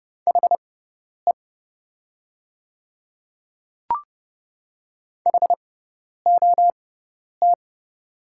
Morse code, heard as a keyed tone, about 30 wpm, spaced as for 10 wpm, 700 Hz. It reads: HE HOT
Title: 6x Character Spacing - Top 100 Words - 30wpm